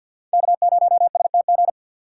Morse code, heard as a keyed tone, about 50 wpm, 700 Hz. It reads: K0STG
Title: USA Call Signs - 5 Characters - 3x Character Spacing - 50wpm